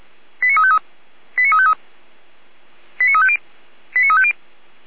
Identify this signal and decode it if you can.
Five-tone selective call (ZVEI), heard as quick runs of five tones, tones of about 70 ms. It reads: 89353 89353 89350 89350